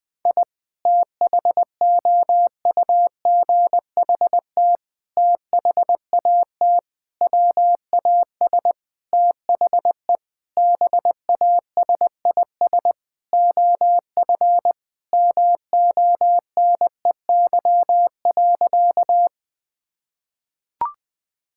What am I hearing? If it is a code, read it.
Morse code, heard as a keyed tone, about 20 wpm, 700 Hz. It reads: I THOUGHT THAT WAS THE BASIS OF MONEY.
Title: Sentences from Top 1000 Words - Review Part 5 - 20wpm